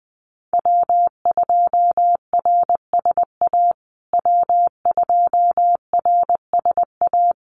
Morse code, heard as a keyed tone, about 20 wpm, 700 Hz. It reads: W2RSA W2RSA
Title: USA Call Signs Repeated 3x 20wpm